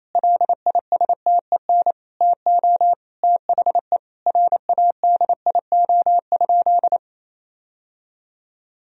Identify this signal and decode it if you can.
Morse code, heard as a keyed tone, about 28 wpm, 700 Hz. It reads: LISTEN TO THE RADIO?
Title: Sentences from Top 1000 Words - Review Part 3 - 28wpm